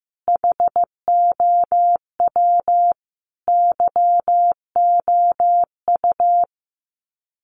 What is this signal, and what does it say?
Morse code, heard as a keyed tone, about 15 wpm, 700 Hz. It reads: HOW YOU
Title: Sets of 5 Words Encore 15wpm